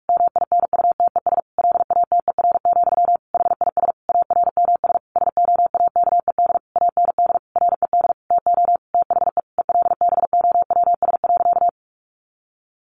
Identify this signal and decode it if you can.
Morse code, heard as a keyed tone, about 44 wpm, 700 Hz. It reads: MINUTES LATER, HIS ARMS SOAKED AND RED TO THE ELBOWS.
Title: A Sound Of Thunder 45wpm